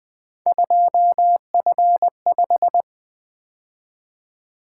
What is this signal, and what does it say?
Morse code, heard as a keyed tone, about 24 wpm, 700 Hz. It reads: 2F5
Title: Speed-Racing - Three Letter-Number Combinations - 20wpm